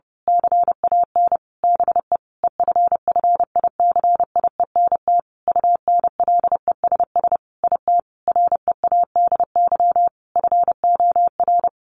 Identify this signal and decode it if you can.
Morse code, heard as a keyed tone, about 28 wpm, 700 Hz. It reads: CAN BE EFFICIENT UNLESS IT READY FOR
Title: Sentences from Top 1300 Words 30wpm